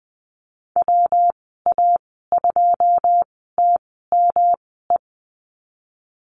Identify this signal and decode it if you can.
Morse code, heard as a keyed tone, about 19 wpm, spaced as for 14 wpm, 700 Hz. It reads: WA2TME
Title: 154 Lesson 25g - Letter p - Call Signs 20wpm